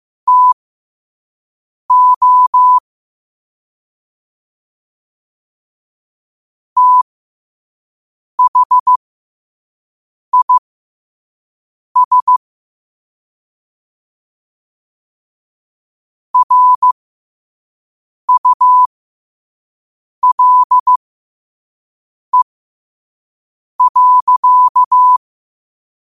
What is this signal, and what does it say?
Morse code, heard as a keyed tone, about 15 wpm, spaced as for 5 wpm, 1000 Hz. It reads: TO THIS RULE.
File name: qotdfiles / 5wpm